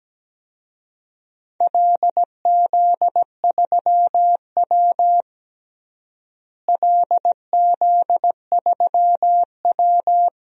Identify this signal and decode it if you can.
Morse code, heard as a keyed tone, about 17 wpm, 700 Hz. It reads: LZ3W LZ3W
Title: DXCC-100 Call Signs - 4 Characters - Repeated 2x - 17wpm